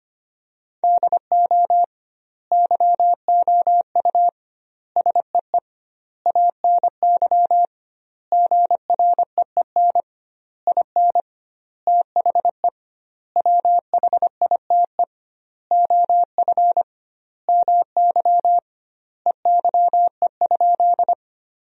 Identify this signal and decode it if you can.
Morse code, heard as a keyed tone, about 25 wpm, 700 Hz. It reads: DO YOU SEE ANY GREEN IN THE WHITE OF MY EYE?
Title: 2x Word-Spacing - Sentences from Top 500 Words - 25wpm